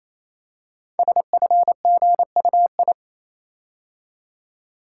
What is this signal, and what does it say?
Morse code, heard as a keyed tone, about 27 wpm, 700 Hz. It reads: SFGUI
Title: Code Groups - 5 Letter:Numbers - 28wpm